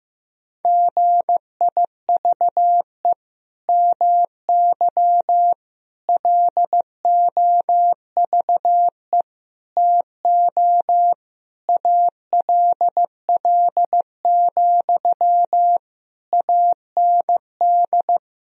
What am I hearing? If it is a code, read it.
Morse code, heard as a keyed tone, about 15 wpm, 700 Hz. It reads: GIVE MY LOVE TO ALL, AND
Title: Sentences from Top 1000 Words - Review Part 4 - 15wpm